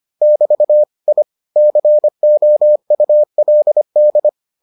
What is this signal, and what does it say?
Morse code, heard as a keyed tone, about 25 wpm, 600 Hz. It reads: <BT> I COULD